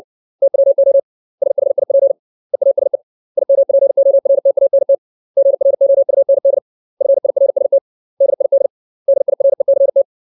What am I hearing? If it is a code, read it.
Morse code, heard as a keyed tone, about 60 wpm, 550 Hz. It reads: TWO SHIP EASE IMPORTANT GROUND FIRST DID DIRECT
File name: 632 words / 632 Words 60WPM